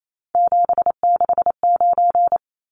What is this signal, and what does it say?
Morse code, heard as a keyed tone, about 28 wpm, 700 Hz. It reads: 769
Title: Numbers - 3-Digit 28wpm